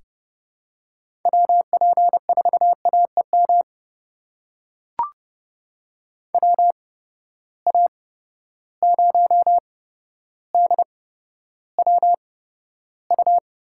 Morse code, heard as a keyed tone, about 30 wpm, spaced as for 13 wpm, 700 Hz. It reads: WP4AEM WA0DWU
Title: USA Call Signs - 6 Characters - 8x Character Spacing - 30wpm